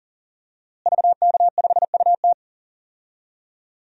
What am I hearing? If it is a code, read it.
Morse code, heard as a keyed tone, about 40 wpm, 700 Hz. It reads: VK5UT
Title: DXCC-100 Call Signs - 5 Characters - Repeated 2x - 40wpm